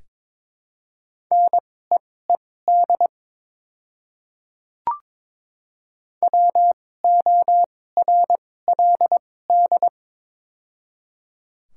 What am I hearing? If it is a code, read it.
Morse code, heard as a keyed tone, about 22 wpm, spaced as for 16 wpm, 700 Hz. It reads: NEED WORLD
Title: 2x Character Spacing - Top 300 Words - 22wpm